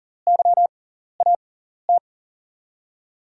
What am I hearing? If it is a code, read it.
Morse code, heard as a keyed tone, about 40 wpm, spaced as for 14 wpm, 700 Hz. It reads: YAT
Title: USA Call Signs - 5 Characters - 6x Character Spacing - 40wpm